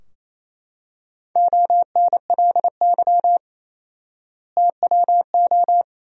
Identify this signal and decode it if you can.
Morse code, heard as a keyed tone, about 27 wpm, 700 Hz. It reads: ONLY TWO
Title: 4x Word-Spacing - Sentences from Top 200 Words - 28wpm